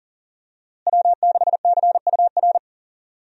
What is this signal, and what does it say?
Morse code, heard as a keyed tone, about 40 wpm, 700 Hz. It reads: W6CUR
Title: USA Call Signs - 5 Characters - Repeated 2x - 40wpm